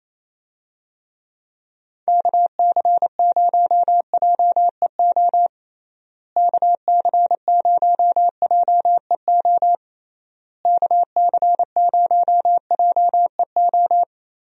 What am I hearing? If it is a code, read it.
Morse code, heard as a keyed tone, about 28 wpm, 700 Hz. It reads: KC0JEO KC0JEO KC0JEO
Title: USA Call Signs - 6 Characters - Repeated 3x - 28wpm